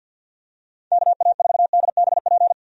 Morse code, heard as a keyed tone, about 45 wpm, 700 Hz. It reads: KA4DBP
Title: USA Call Signs - 6 Characters - 50wpm